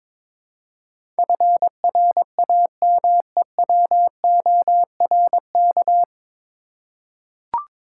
Morse code, heard as a keyed tone, about 22 wpm, 700 Hz. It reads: FRAMEWORK
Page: Morse Code Ninja practice 22wpm